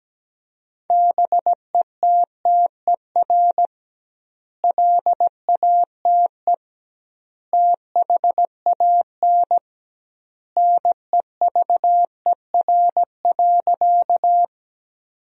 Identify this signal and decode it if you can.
Morse code, heard as a keyed tone, about 17 wpm, 700 Hz. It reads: BETTER LATE THAN NEVER.
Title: 2x Word-Spacing - Sentences from Top 400 Words - 17wpm